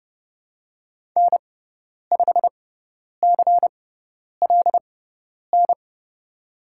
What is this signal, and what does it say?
Morse code, heard as a keyed tone, about 30 wpm, spaced as for 10 wpm, 700 Hz. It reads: N5CLN
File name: Lesson 16g - Letter c - Call Signs 30wpm